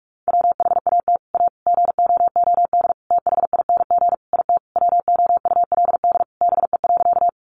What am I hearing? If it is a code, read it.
Morse code, heard as a keyed tone, about 44 wpm, 700 Hz. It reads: WHAT A GOOD THING IT WOULD BE.
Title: Sentences from Top 200 Words 45wpm